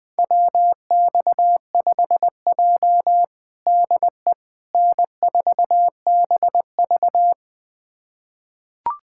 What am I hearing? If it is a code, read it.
Morse code, heard as a keyed tone, about 19 wpm, 700 Hz. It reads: WX5J DE N4BV
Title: USA Callsign DE Callsign - 4 Characters - 20wpm